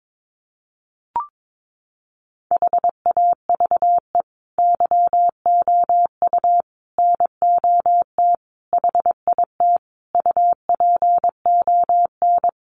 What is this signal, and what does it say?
Morse code, heard as a keyed tone, about 22 wpm, 700 Hz. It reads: HAVE YOU NOT HIT UPON
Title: Sentences from Top 1200 Words 22wpm